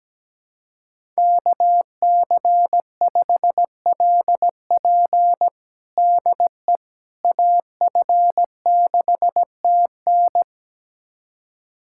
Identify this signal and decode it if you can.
Morse code, heard as a keyed tone, about 17 wpm, 700 Hz. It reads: KC5LP DE AF6TN
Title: USA Callsign DE Callsign - 5 Characters - 17wpm